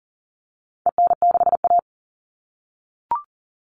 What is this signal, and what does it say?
Morse code, heard as a keyed tone, about 40 wpm, 700 Hz. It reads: EN6A
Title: All Cut Numbers - 4-Digit 40wpm